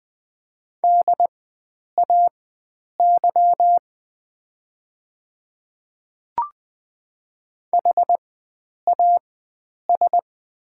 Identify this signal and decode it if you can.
Morse code, heard as a keyed tone, about 20 wpm, spaced as for 9 wpm, 690 Hz. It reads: DAY HAS
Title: 4x Character Spacing - Top 100 Words - 20wpm